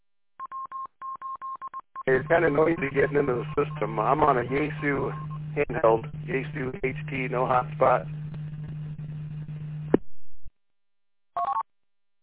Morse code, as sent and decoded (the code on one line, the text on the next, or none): none